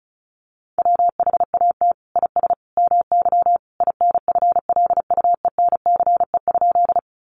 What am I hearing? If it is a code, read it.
Morse code, heard as a keyed tone, about 34 wpm, 700 Hz. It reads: WHAT IS MY INFLUENCE?
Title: Sentences from Top 1100 Words 35wpm